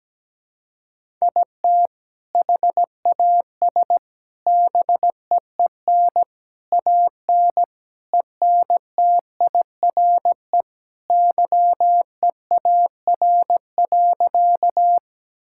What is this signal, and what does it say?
Morse code, heard as a keyed tone, about 17 wpm, 700 Hz. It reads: IT HAS BEEN AN ENTIRE YEAR.
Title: Sentences from Top 1000 Words - Review Part 5 - 17wpm